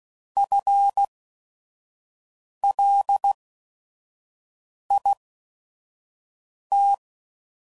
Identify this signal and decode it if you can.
Morse code, heard as a keyed tone, about 16 wpm, spaced as for 5 wpm, 800 Hz. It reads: FLIT